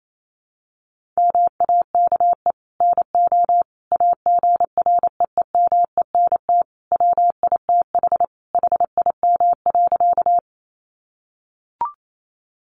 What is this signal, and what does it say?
Morse code, heard as a keyed tone, about 28 wpm, 700 Hz. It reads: MAKE NO AGREEMENT WITH HIM.
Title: Sentences from Top 1500 Words 28wpm